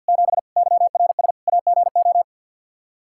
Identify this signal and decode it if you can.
Morse code, heard as a keyed tone, about 51 wpm, 700 Hz. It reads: 7 YRS AGO
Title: K1USN SST QSO Practice 50wpm